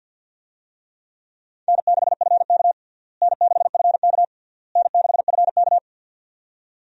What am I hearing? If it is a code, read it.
Morse code, heard as a keyed tone, about 46 wpm, 700 Hz. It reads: N6FK N6FK N6FK
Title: USA Call Signs - 4 Characters - Repeated 3x - 50wpm